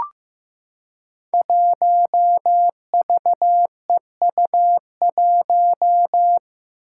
Morse code, heard as a keyed tone, about 15 wpm, 700 Hz. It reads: 1VEU1
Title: Five Challenging Letter-Number Combinations 15wpm